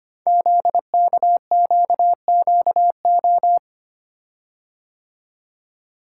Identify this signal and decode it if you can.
Morse code, heard as a keyed tone, about 26 wpm, 700 Hz. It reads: ZKQQO
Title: Code Groups - 5 Letters - 25wpm